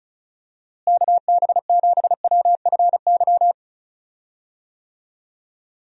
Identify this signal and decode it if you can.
Morse code, heard as a keyed tone, about 36 wpm, 700 Hz. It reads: KB7WFY
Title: USA Call Signs - 6 Characters - Repeated 2x - 35wpm